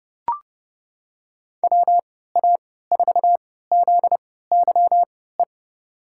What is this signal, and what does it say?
Morse code, heard as a keyed tone, about 30 wpm, spaced as for 17 wpm, 700 Hz. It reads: WA4ZYE
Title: USA Call Signs - 6 Characters - 3x Character Spacing - 30wpm